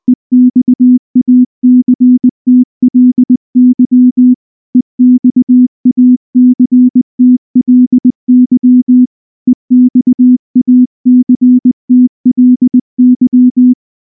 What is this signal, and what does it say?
Morse code, heard as a keyed tone, about 19 wpm, 250 Hz. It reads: EXACTLY EXACTLY EXACTLY
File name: morse / morse-training-20WPM